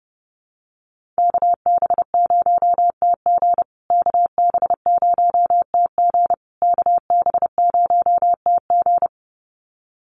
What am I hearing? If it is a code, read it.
Morse code, heard as a keyed tone, about 30 wpm, 700 Hz. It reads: KB0TG KB0TG KB0TG
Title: USA Call Signs Repeated 3x 30wpm